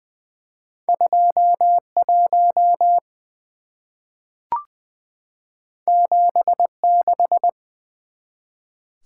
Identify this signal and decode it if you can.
Morse code, heard as a keyed tone, about 20 wpm, 700 Hz. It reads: 21 76
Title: Two Letter-Number Combinations - 20wpm